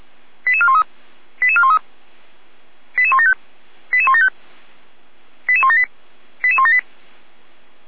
Five-tone selective call (ZVEI), heard as quick runs of five tones, tones of about 70 ms. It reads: 88413 88413 88175 88175 88178 88178